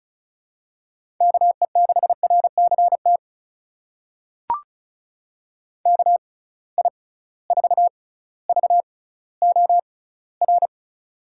Morse code, heard as a keyed tone, about 36 wpm, spaced as for 18 wpm, 700 Hz. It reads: KE6RCT KI4VOR
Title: USA Call Signs - 6 Characters - 6x Character Spacing - 35wpm